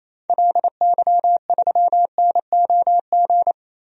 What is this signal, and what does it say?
Morse code, heard as a keyed tone, about 28 wpm, 700 Hz. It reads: LY3NOG
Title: DXCC-100 Call Signs - 6 Characters - 28wpm